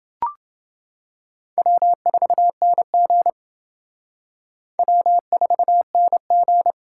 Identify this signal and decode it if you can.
Morse code, heard as a keyed tone, about 29 wpm, 700 Hz. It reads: W4NG W4NG